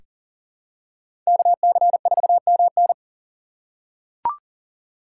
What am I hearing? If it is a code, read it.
Morse code, heard as a keyed tone, about 38 wpm, 700 Hz. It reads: KC4MN
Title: USA Call Signs - 5 Characters - 3x Character Spacing - 40wpm